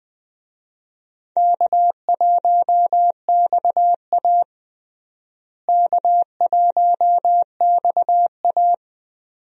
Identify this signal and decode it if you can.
Morse code, heard as a keyed tone, about 20 wpm, 700 Hz. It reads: K1XA K1XA
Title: USA Call Signs - 4 Characters - Repeated 2x - 20wpm